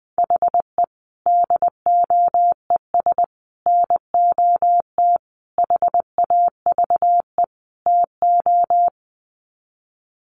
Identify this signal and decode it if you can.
Morse code, heard as a keyed tone, about 20 wpm, 700 Hz. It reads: HE DOES NOT HAVE TO
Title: Sets of 5 Words Encore 20wpm